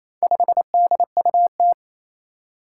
Morse code, heard as a keyed tone, about 27 wpm, 700 Hz. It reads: 5DUT
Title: Four Challenging Letter-Number Combinations 28wpm